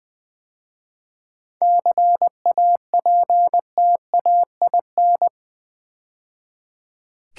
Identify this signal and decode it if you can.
Morse code, heard as a keyed tone, about 20 wpm, 700 Hz. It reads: CAPTAIN
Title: 7 Letter Words - First 3 Letters Given - 20wpm